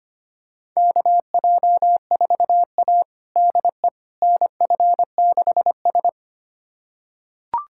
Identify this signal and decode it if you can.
Morse code, heard as a keyed tone, about 25 wpm, 700 Hz. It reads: KJ4A DE NF6S
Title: USA Callsign DE Callsign - 4 Characters - 25wpm